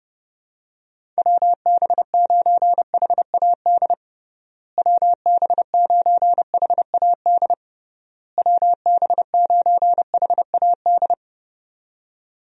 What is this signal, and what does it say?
Morse code, heard as a keyed tone, about 30 wpm, 700 Hz. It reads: WB9HAD WB9HAD WB9HAD
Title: USA Call Signs - 6 Characters - Repeated 3x - 30wpm